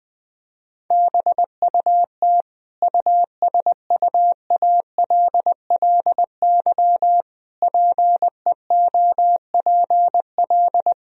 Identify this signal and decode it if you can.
Morse code, heard as a keyed tone, about 20 wpm, 700 Hz. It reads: BUT USUALLY PEOPL
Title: Sentences from Top 1000 Words - Review Part 3 - 20wpm